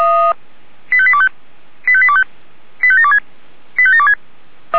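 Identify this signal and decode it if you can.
Five-tone selective call (ZVEI), heard as quick runs of five tones, tones of about 70 ms. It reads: 85825 85825 85726 85726
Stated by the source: ZVEI five-tone selective call, code 85758, received with an SDR.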